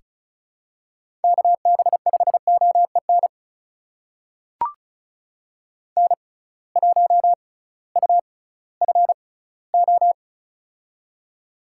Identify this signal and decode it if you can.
Morse code, heard as a keyed tone, about 36 wpm, spaced as for 19 wpm, 700 Hz. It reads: KB5OEN N1UFO
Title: USA Call Signs - 6x Character Spacing - 35wpm